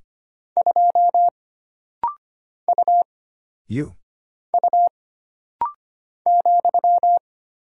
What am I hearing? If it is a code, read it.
Morse code, heard as a keyed tone, about 25 wpm, 700 Hz. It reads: 2 U U ,
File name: Single Character (Letter:Number:Punctuation:Prosign) - Rapid-Fire - 25wpm